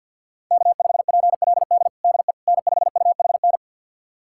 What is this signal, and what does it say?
Morse code, heard as a keyed tone, about 50 wpm, 700 Hz. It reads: K5PLD DE N5UHN
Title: USA Callsign DE Callsign - 5 Characters - 50wpm